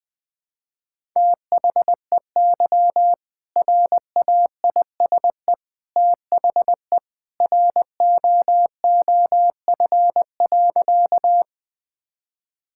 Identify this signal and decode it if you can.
Morse code, heard as a keyed tone, about 20 wpm, 700 Hz. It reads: THEY RAISE THE ROOF.
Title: All-Age Appropriate Jokes - 20wpm